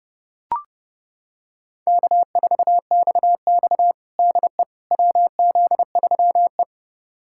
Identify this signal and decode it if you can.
Morse code, heard as a keyed tone, about 30 wpm, 700 Hz. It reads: K4XX DE WZ3E
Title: USA Callsign DE Callsign - 4 Characters - 30wpm